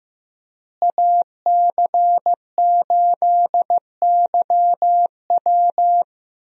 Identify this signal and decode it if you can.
Morse code, heard as a keyed tone, about 15 wpm, 700 Hz. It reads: AC8YW
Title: USA Call Signs - 5 Characters - Repeated 2x - 15wpm